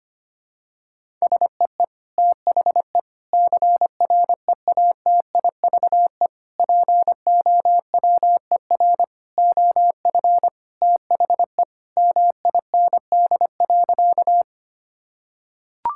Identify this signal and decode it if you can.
Morse code, heard as a keyed tone, about 25 wpm, 700 Hz. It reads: SEE THE CREATIVE POWER OF THE MIND.
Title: Sentences from Top 1000 Words - Review Part 5 - 25wpm